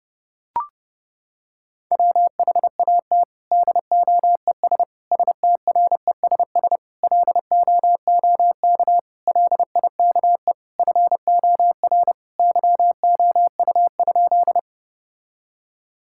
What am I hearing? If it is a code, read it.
Morse code, heard as a keyed tone, about 30 wpm, 700 Hz. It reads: WHAT DOES STRESS LOOK LIKE FOR YOU?
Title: Sentences from Top 1000 Words - Review Part 6 - 30wpm